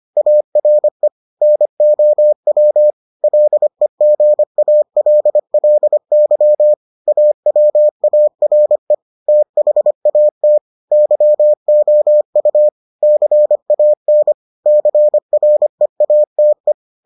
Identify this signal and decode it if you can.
Morse code, heard as a keyed tone, about 25 wpm, 600 Hz. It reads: ARE NOW LEGALLY AWARE THAT YOU CAN CREATE